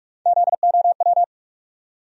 Morse code, heard as a keyed tone, about 46 wpm, 700 Hz. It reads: ZOW
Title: USA Call Sign Postfixes - 3 Letters 45wpm